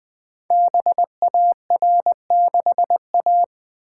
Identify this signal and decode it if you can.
Morse code, heard as a keyed tone, about 20 wpm, 700 Hz. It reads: BAR6A
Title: Code Groups - 5 Letter:Numbers - 20wpm